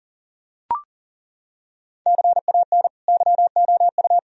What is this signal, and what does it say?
Morse code, heard as a keyed tone, about 40 wpm, 700 Hz. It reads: CAN YOU
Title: Sentences from Top 1000 Words - Review Part 2 - 40wpm